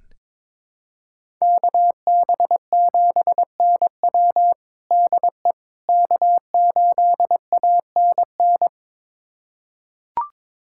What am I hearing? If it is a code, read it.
Morse code, heard as a keyed tone, about 22 wpm, 700 Hz. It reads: KB7NW DE K8ANN
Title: USA Callsign DE Callsign - 5 Characters - 22wpm